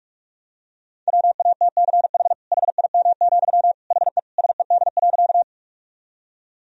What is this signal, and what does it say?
Morse code, heard as a keyed tone, about 46 wpm, 700 Hz. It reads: WATCH HIM, HE SED.
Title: Sentences from Top 300 Words and Abbreviations 45wpm